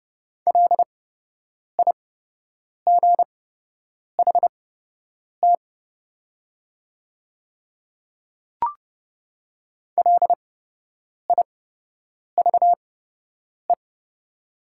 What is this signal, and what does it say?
Morse code, heard as a keyed tone, about 30 wpm, spaced as for 8 wpm, 700 Hz. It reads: LIGHT LIVE
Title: 8x Character Spacing - Top 300 Words - 30wpm